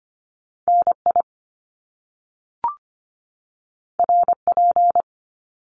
Morse code, heard as a keyed tone, about 25 wpm, 700 Hz. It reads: NI RP
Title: Top Two Letter Combinations in Words 25wpm